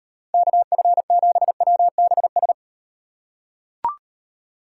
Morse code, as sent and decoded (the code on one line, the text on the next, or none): -.- ..-. --... .-- -... ...
KF7WBS